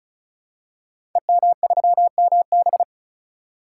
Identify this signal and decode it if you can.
Morse code, heard as a keyed tone, about 36 wpm, 700 Hz. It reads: EM3MB
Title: Code Groups - 5 Letter:Numbers - 35wpm